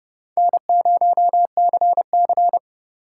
Morse code, heard as a keyed tone, about 39 wpm, 700 Hz. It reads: N0CC